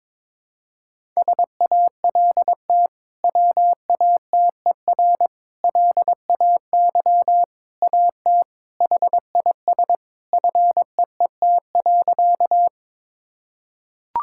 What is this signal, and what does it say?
Morse code, heard as a keyed tone, about 22 wpm, 700 Hz. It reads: SALT WATER LAY AT HIS FEET.
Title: Sentences from Top 1000 Words - Review Part 5 - 22wpm